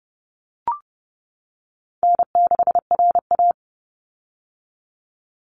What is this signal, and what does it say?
Morse code, heard as a keyed tone, about 30 wpm, 700 Hz. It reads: N6RA
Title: Easy Call Signs - 30wpm